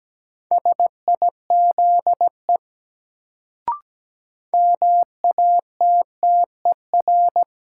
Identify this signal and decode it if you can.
Morse code, heard as a keyed tone, about 17 wpm, 700 Hz. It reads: SIZE MATTER
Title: Sets of 3 Words - New Word Drill - 17wpm